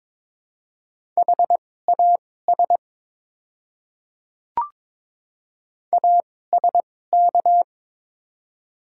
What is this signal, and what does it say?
Morse code, heard as a keyed tone, about 23 wpm, spaced as for 16 wpm, 700 Hz. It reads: HAS ASK
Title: 2x Character Spacing - Top 200 Words - 22wpm